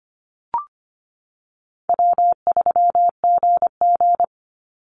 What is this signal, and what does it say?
Morse code, heard as a keyed tone, about 25 wpm, 700 Hz. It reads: W3GG
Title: Call Signs - 25wpm